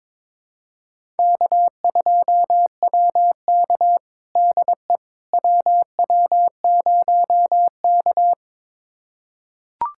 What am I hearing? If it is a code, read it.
Morse code, heard as a keyed tone, about 22 wpm, 700 Hz. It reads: K2WK DE WW0K
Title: USA Callsign DE Callsign - 4 Characters - 22wpm